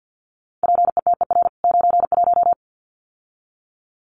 Morse code, heard as a keyed tone, about 50 wpm, 700 Hz. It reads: PETER 91
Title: CWT Exchange 50wpm